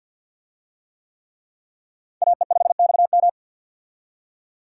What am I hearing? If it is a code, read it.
Morse code, heard as a keyed tone, about 47 wpm, 700 Hz. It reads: AE5XM
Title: USA Call Signs - 5 Characters - 50wpm